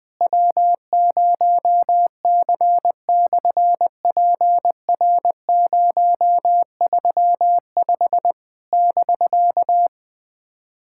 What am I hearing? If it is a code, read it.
Morse code, heard as a keyed tone, about 20 wpm, 700 Hz. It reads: W0C/PR035 <BK>
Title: SOTA QSO Practice 20wpm